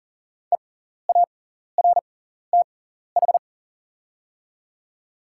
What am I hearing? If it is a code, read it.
Morse code, heard as a keyed tone, about 43 wpm, spaced as for 14 wpm, 700 Hz. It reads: EARTH
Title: 6x Character Spacing - Top 200 Words - 40wpm